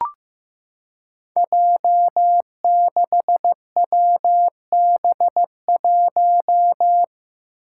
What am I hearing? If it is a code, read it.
Morse code, heard as a keyed tone, about 15 wpm, 700 Hz. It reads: J6WB1
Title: Five Challenging Letter-Number Combinations 15wpm